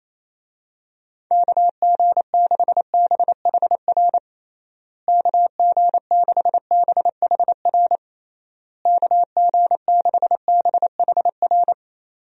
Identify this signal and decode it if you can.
Morse code, heard as a keyed tone, about 28 wpm, 700 Hz. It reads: KG6BHR KG6BHR KG6BHR
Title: USA Call Signs - 6 Characters - Repeated 3x - 28wpm